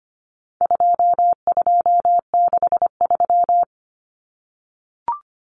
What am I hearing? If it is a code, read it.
Morse code, heard as a keyed tone, about 25 wpm, 700 Hz. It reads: 2263
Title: Common Cut Numbers - 4-Digit 25wpm